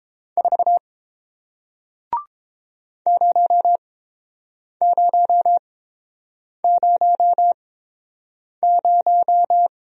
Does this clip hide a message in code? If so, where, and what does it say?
Yes: Morse code from 0.4 to 9.7 s, heard as a keyed tone, about 30 wpm, spaced as for 7 wpm, 700 Hz. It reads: 4 0000